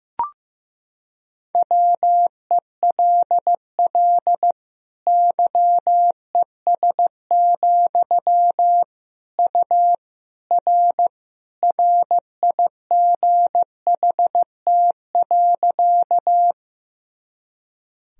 Morse code, heard as a keyed tone, about 15 wpm, 700 Hz. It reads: WELL YES, U R RIGHT.
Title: Sentences from Top 500 Words and Abbreviations 15wpm